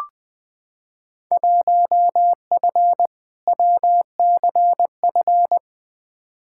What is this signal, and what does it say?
Morse code, heard as a keyed tone, about 20 wpm, 700 Hz. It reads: 1F WCF
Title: Field Day Exchanges - 20wpm